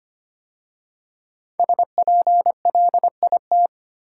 Morse code, heard as a keyed tone, about 25 wpm, 700 Hz. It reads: SPLIT
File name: 5 Letter Words - First 3 Letters Given - 25wpm